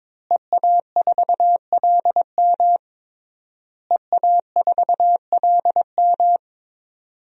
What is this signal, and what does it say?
Morse code, heard as a keyed tone, about 22 wpm, 700 Hz. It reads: EA4LM EA4LM
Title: DXCC-100 Call Signs - 5 Characters - Repeated 3x - 22wpm